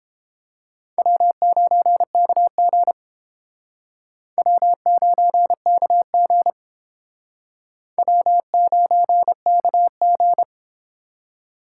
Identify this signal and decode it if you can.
Morse code, heard as a keyed tone, about 30 wpm, 700 Hz. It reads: W9KG W9KG W9KG